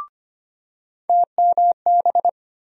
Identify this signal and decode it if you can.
Morse code, heard as a keyed tone, about 25 wpm, 700 Hz. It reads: TMB
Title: USA Call Sign Postfixes - 3 Letters 25wpm